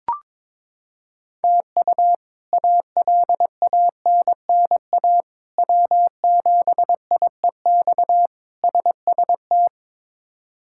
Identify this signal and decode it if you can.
Morse code, heard as a keyed tone, about 22 wpm, 700 Hz. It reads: TU ALANNA W7IEX SST
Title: K1USN SST QSO Practice - 22wpm